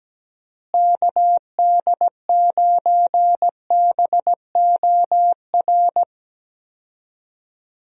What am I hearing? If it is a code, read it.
Morse code, heard as a keyed tone, about 17 wpm, 700 Hz. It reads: KD9BOR
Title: USA Call Signs - 6 Characters - Repeated 3x - 17wpm